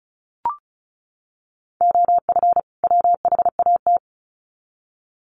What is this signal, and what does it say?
Morse code, heard as a keyed tone, about 34 wpm, 700 Hz. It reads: OF WHAT
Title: Sets of 2 Words 35wpm